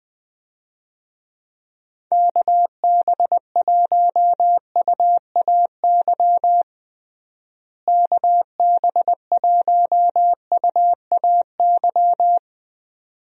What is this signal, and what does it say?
Morse code, heard as a keyed tone, about 21 wpm, 700 Hz. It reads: KB1UAY KB1UAY
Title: USA Call Signs - 6 Characters - Repeated 2x - 20wpm